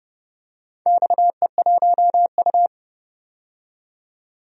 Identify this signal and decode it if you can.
Morse code, heard as a keyed tone, about 30 wpm, 700 Hz. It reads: XE1U
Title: DXCC-100 Call Signs - 4 Characters - 30wpm